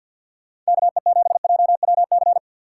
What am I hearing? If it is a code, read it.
Morse code, heard as a keyed tone, about 60 wpm, 700 Hz. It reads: KE7JWC
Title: Speed Racing - USA 2x3 Call Signs 50wpm